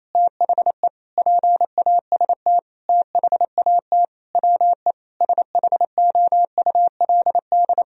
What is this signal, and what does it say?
Morse code, heard as a keyed tone, about 28 wpm, 700 Hz. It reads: THE PAST THAT WE SHOULD
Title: Sentences from Top 1000 Words - Review Part 4 - 28wpm